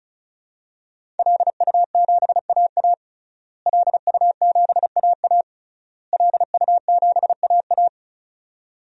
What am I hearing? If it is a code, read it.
Morse code, heard as a keyed tone, about 36 wpm, 700 Hz. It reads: LU7AA LU7AA LU7AA
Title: DXCC-100 Call Signs - 5 Characters - Repeated 3x - 35wpm